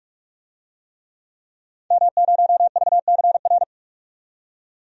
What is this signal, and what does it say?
Morse code, heard as a keyed tone, about 42 wpm, 700 Hz. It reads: M0VCR
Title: DXCC-100 Call Signs - 5 Characters - 45wpm